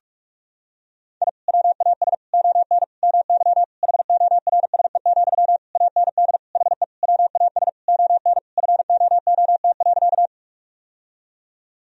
Morse code, heard as a keyed tone, about 43 wpm, 700 Hz. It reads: I WAS ON MY HORSE, AND HE WAS ON FOOT.
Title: Sentences from Top 1000 Words - Review Part 3 - 45wpm